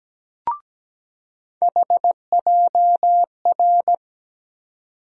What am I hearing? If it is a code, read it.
Morse code, heard as a keyed tone, about 17 wpm, 700 Hz. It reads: HJR
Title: USA Call Sign Postfixes - 3 Letters 17wpm